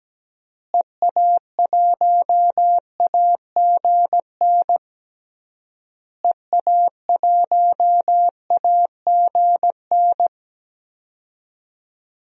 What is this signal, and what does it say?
Morse code, heard as a keyed tone, about 17 wpm, 700 Hz. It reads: EA1AGN EA1AGN
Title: DXCC-100 Call Signs - 6 Characters - Repeated 3x - 17wpm